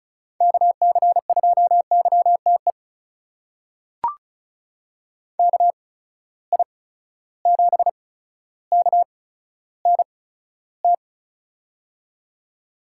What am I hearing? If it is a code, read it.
Morse code, heard as a keyed tone, about 36 wpm, spaced as for 15 wpm, 700 Hz. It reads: KC2YTE KI7KNT
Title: USA Call Signs - 6 Characters - 8x Character Spacing - 35wpm